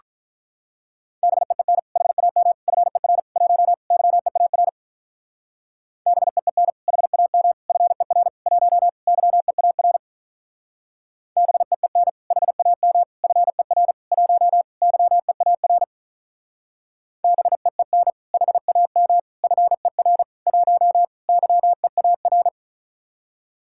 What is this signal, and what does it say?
Morse code, heard as a keyed tone, about 44 wpm, 700 Hz. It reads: BEEN HAM FER 1 YEAR BEEN HAM FER 1 YEAR BEEN HAM FER 1 YEAR BEEN HAM FER 1 YEAR